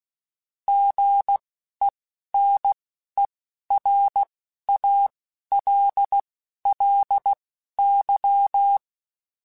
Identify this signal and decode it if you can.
Morse code, heard as a keyed tone, about 16 wpm, spaced as for 12 wpm, 800 Hz. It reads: GENERALLY
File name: Weather Sets / Weather Sets-mp3s / Weather 5 1 @10